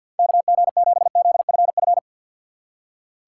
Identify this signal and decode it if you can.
Morse code, heard as a keyed tone, about 64 wpm, 700 Hz. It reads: KG7ZFF